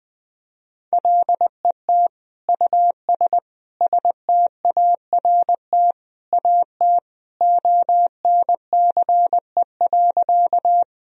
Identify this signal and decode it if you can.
Morse code, heard as a keyed tone, about 20 wpm, 700 Hz. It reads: LET US START AT ONCE.
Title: Sentences from Top 1000 Words - Review Part 1 - 20wpm